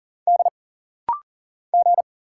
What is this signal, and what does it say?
Morse code, heard as a keyed tone, about 40 wpm, 700 Hz. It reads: D G
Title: Single Letter-Number Alt - Rapid-Fire 40wpm